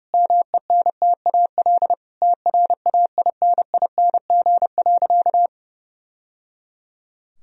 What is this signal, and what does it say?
Morse code, heard as a keyed tone, about 30 wpm, 700 Hz. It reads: MENTAL TRAINING.
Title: Sentences from Top 1000 Words - Review Part 4 - 30wpm